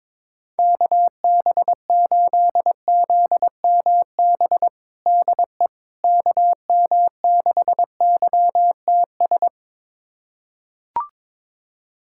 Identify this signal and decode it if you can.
Morse code, heard as a keyed tone, about 22 wpm, 700 Hz. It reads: KB8ZMB DE KM6YTS
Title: USA Callsign DE Callsign - 6 Characters - 22wpm